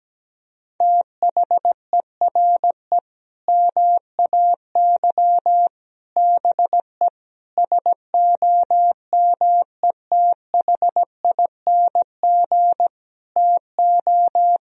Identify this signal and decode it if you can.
Morse code, heard as a keyed tone, about 17 wpm, 700 Hz. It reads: THERE MAY BE SOMETHING TO
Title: Sentences from Top 1000 Words - Review Part 3 - 17wpm